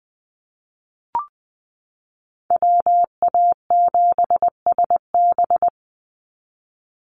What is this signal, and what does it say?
Morse code, heard as a keyed tone, about 20 wpm, 700 Hz. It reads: WA7SB
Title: NA SOTA Chaser Callsigns 20wpm